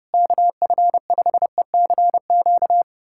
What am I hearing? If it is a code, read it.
Morse code, heard as a keyed tone, about 32 wpm, 700 Hz. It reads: KF5ECQ
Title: USA Callsign DE Callsign - 6 Characters - 30wpm